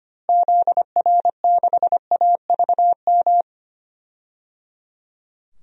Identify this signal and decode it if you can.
Morse code, heard as a keyed tone, about 25 wpm, 700 Hz. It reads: ZR6AVM
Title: DXCC-100 Call Signs - 6 Characters - Repeated 3x - 25wpm